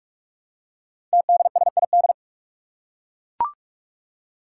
Morse code, heard as a keyed tone, about 44 wpm, 700 Hz. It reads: TDSID
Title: Five Challenging Letter-Number Combinations 45wpm